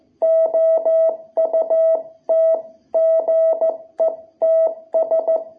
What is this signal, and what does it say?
Morse code, heard as a keyed tone, about 14 wpm, 660 Hz. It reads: OUT GETS
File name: jh8cbh / 2021cw